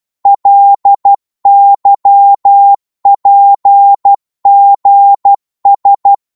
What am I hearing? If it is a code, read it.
Morse code, heard as a keyed tone, about 12 wpm, 800 Hz. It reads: LYPGS
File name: Morse / ltr-12wpm